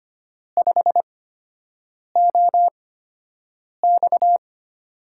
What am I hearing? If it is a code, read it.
Morse code, heard as a keyed tone, about 25 wpm, spaced as for 7 wpm, 700 Hz. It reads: 5OX
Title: USA Call Signs - 8x Character Spacing - 25wpm